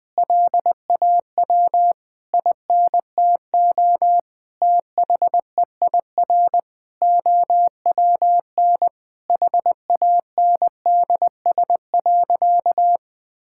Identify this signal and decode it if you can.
Morse code, heard as a keyed tone, about 20 wpm, 700 Hz. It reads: LAW INTO THEIR OWN HANDS.
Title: Sentences with 8-Word Phrases - Spoken Beginning - 20wpm